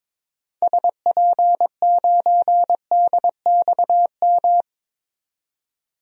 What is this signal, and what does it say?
Morse code, heard as a keyed tone, about 23 wpm, 700 Hz. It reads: SP9DXM
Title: DXCC-100 Call Signs - 6 Characters - 22wpm